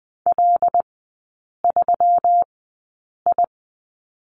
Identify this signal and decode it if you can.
Morse code, heard as a keyed tone, about 20 wpm, spaced as for 8 wpm, 690 Hz. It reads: L3I
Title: 166 Lesson 27e - Number 7 - Three Characters 20wpm